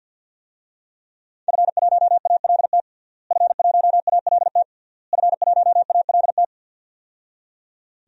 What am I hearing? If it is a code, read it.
Morse code, heard as a keyed tone, about 48 wpm, 700 Hz. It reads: F1ALT F1ALT F1ALT
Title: DXCC-100 Call Signs - 5 Characters - Repeated 3x - 50wpm